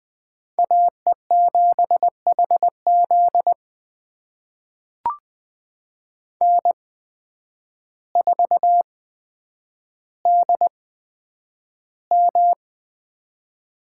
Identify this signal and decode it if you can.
Morse code, heard as a keyed tone, about 20 wpm, spaced as for 9 wpm, 700 Hz. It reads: AE7HZ N4DM
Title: USA Call Signs - 5 Characters - 8x Character Spacing - 20wpm